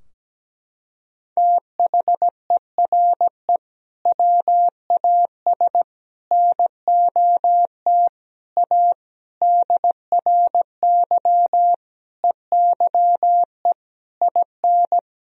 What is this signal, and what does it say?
Morse code, heard as a keyed tone, about 17 wpm, 700 Hz. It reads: THERE WAS NOT A DRY EYE IN
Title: Sentences from Top 1000 Words - Review Part 3 - 17wpm